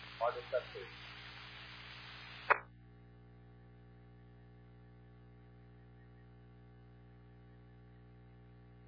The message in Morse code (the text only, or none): none